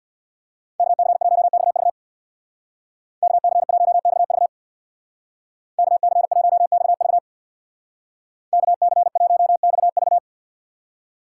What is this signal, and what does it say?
Morse code, heard as a keyed tone, about 61 wpm, 700 Hz. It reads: KC1XV KC1XV KC1XV KC1XV
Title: Speed Racing - USA 2x2 Call Signs 50wpm